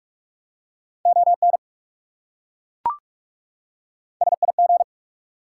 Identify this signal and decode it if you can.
Morse code, heard as a keyed tone, about 38 wpm, 700 Hz. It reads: ON SIG